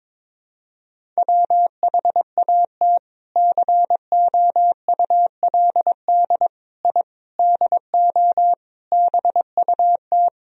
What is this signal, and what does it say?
Morse code, heard as a keyed tone, about 22 wpm, 700 Hz. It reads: WHAT COULD I DO BUT
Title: Sentences from Top 1000 Words - Review Part 1 - 22wpm